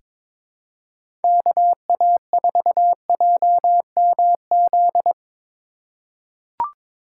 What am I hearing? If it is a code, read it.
Morse code, heard as a keyed tone, about 22 wpm, 700 Hz. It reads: KA4JMZ
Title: USA Call Signs - 6 Characters - 2x Character Spacing - 22wpm